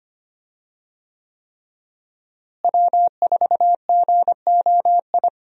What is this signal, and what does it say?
Morse code, heard as a keyed tone, about 25 wpm, 700 Hz. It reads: W4GOI
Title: USA Call Signs - 5 Characters - Repeated 3x - 25wpm